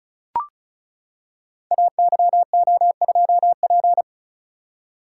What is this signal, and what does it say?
Morse code, heard as a keyed tone, about 36 wpm, 700 Hz. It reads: AYO2P
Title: Code Groups - 5 Letter:Numbers - 35wpm